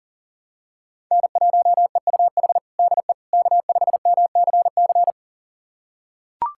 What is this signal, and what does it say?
Morse code, heard as a keyed tone, about 40 wpm, 700 Hz. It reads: N1EUH DE K5MCC
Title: USA Callsign DE Callsign - 5 Characters - 40wpm